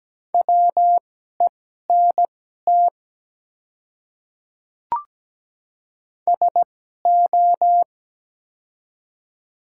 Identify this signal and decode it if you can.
Morse code, heard as a keyed tone, about 17 wpm, spaced as for 12 wpm, 700 Hz. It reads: WENT SO
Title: 2x Character Spacing - Top 200 Words - 17wpm